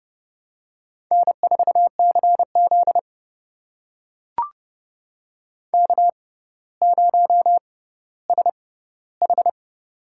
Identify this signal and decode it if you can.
Morse code, heard as a keyed tone, about 30 wpm, spaced as for 15 wpm, 700 Hz. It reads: N4CZ K0SH
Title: USA Call Signs - 4 Characters - 6x Character Spacing - 30wpm